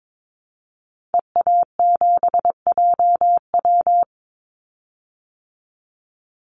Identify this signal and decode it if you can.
Morse code, heard as a keyed tone, about 22 wpm, 700 Hz. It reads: EA7JW
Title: Call Signs Repeated 2x - 22wpm